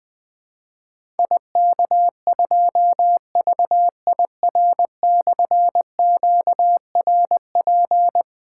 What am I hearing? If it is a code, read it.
Morse code, heard as a keyed tone, about 20 wpm, 700 Hz. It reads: IK2VIR/QRP
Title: Difficult Call Signs 20wpm